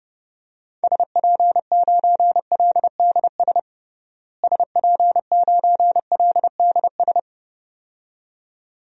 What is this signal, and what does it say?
Morse code, heard as a keyed tone, about 30 wpm, 700 Hz. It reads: SP9LDS SP9LDS
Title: DXCC-100 Call Signs - 6 Characters - Repeated 2x - 30wpm